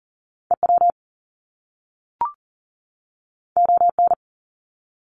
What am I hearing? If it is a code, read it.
Morse code, heard as a keyed tone, about 40 wpm, 700 Hz. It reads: EW ON
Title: Top Two Letter Combinations in Words 40wpm